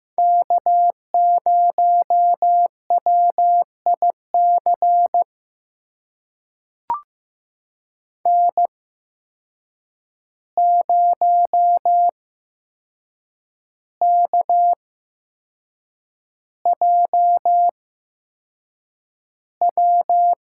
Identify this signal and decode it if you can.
Morse code, heard as a keyed tone, about 15 wpm, spaced as for 6 wpm, 700 Hz. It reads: K0WIC N0KJW
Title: USA Call Signs - 5 Characters - 8x Character Spacing - 15wpm